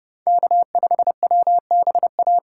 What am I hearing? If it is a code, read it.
Morse code, heard as a keyed tone, about 29 wpm, 700 Hz. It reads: K5WBA